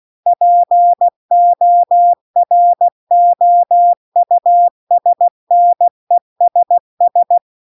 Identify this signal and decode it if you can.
Morse code, heard as a keyed tone, about 16 wpm, 700 Hz. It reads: POROUSNESS
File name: CW-The-Big-Twoheaded-River-2-16WPM